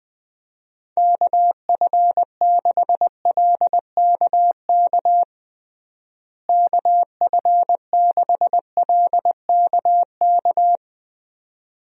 Morse code, heard as a keyed tone, about 20 wpm, 700 Hz. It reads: KF6LKK KF6LKK
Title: USA Call Signs - 6 Characters - Repeated 2x - 20wpm